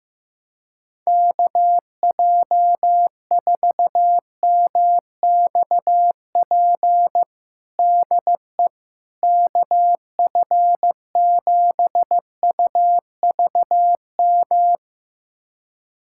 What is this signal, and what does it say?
Morse code, heard as a keyed tone, about 15 wpm, 700 Hz. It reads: KJ4MXP DE KF7UVM
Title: USA Callsign DE Callsign 15wpm